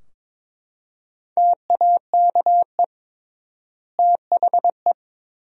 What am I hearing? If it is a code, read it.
Morse code, heard as a keyed tone, about 22 wpm, 700 Hz. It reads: TAKE THE
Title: 3x Word-Spacing - Sentences from Top 400 Words - 22wpm